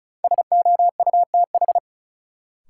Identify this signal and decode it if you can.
Morse code, heard as a keyed tone, about 36 wpm, 700 Hz. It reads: SOUTH
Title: Sentences from Top 400 Words - New Word Drill - 35wpm